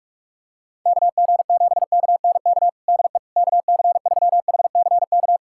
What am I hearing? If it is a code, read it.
Morse code, heard as a keyed tone, about 41 wpm, 700 Hz. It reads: KG7KNK DE KC3HCK
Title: USA Callsign DE Callsign - 6 Characters - 45wpm